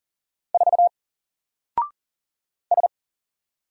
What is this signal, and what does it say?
Morse code, heard as a keyed tone, about 40 wpm, 700 Hz. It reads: 4 S
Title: Single Letter-Number Alt 40wpm